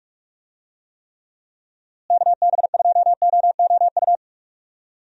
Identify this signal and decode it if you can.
Morse code, heard as a keyed tone, about 42 wpm, 700 Hz. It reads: KB2OOU
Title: USA Call Signs - 6 Characters - Repeated 2x - 45wpm